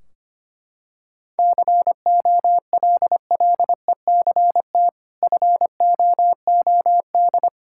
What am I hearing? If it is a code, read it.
Morse code, heard as a keyed tone, about 25 wpm, 700 Hz. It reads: COLLECT FOOD
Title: Sentences from Top 1600 Words 25wpm